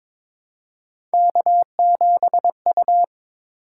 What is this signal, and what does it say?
Morse code, heard as a keyed tone, about 22 wpm, 700 Hz. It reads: K7U
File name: practice / 22wpm / Three Letter-Number Combinations - 22wpm